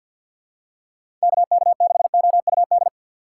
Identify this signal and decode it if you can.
Morse code, heard as a keyed tone, about 47 wpm, 700 Hz. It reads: KK6OUD
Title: USA Call Signs - 6x Character Spacing - 50wpm